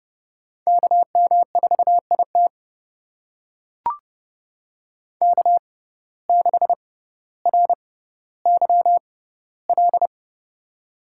Morse code, heard as a keyed tone, about 32 wpm, spaced as for 16 wpm, 700 Hz. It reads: KM4IT K6RYL
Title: USA Call Signs - 5 Characters - 6x Character Spacing - 30wpm